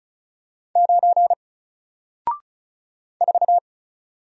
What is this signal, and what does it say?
Morse code, heard as a keyed tone, about 35 wpm, 700 Hz. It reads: 9 4
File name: Single-Letter-Number-35wpm